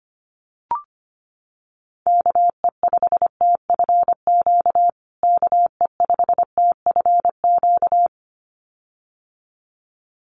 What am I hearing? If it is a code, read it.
Morse code, heard as a keyed tone, about 25 wpm, 700 Hz. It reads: KE5TFQ KE5TFQ
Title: USA Call Signs Repeated 2x 25wpm